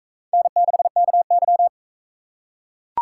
Morse code, heard as a keyed tone, about 44 wpm, 700 Hz. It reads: N6KY